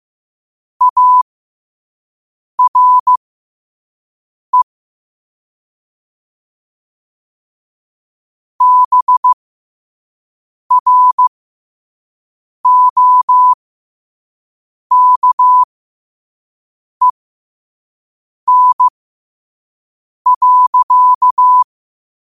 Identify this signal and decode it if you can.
Morse code, heard as a keyed tone, about 15 wpm, spaced as for 5 wpm, 1000 Hz. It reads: ARE BROKEN.
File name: qotdfiles / 5wpm